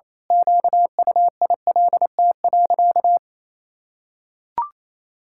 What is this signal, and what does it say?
Morse code, heard as a keyed tone, about 27 wpm, 700 Hz. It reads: QUILT.
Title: Pangram Challenge v2 - 28wpm